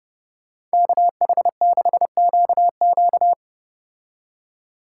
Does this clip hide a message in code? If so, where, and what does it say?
Yes: Morse code from 0.7 to 3.3 s, heard as a keyed tone, about 33 wpm, 700 Hz. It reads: KH6QQ